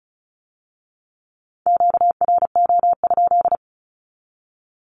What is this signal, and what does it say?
Morse code, heard as a keyed tone, about 36 wpm, 700 Hz. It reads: QRO?
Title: Q-Codes with Spoken Meaning - 35wpm